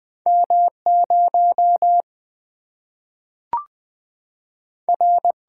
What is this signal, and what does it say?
Morse code, heard as a keyed tone, about 20 wpm, 700 Hz. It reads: M0 R
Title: Single-Letter Single-Number 20wpm